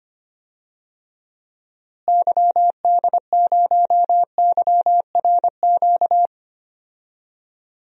Morse code, heard as a keyed tone, about 25 wpm, 700 Hz. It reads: YD0YRQ
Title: DXCC-100 Call Signs - 6 Characters - Repeated 3x - 25wpm